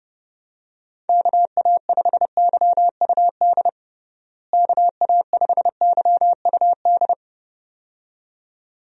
Morse code, heard as a keyed tone, about 30 wpm, 700 Hz. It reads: KA5YUD KA5YUD
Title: USA Call Signs - 6 Characters - Repeated 2x - 30wpm